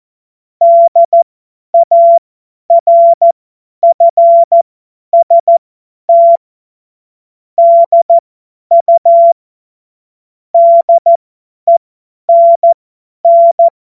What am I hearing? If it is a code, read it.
Morse code, heard as a keyed tone, about 14 wpm, spaced as for 10 wpm, 680 Hz. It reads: DARFST DU DENN
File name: Morsekurs / Witze / Witz 01. März 2023 50 F70